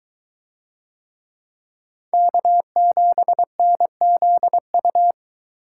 Morse code, heard as a keyed tone, about 22 wpm, 700 Hz. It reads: K7NZU